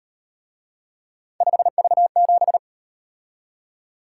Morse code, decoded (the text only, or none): 5V7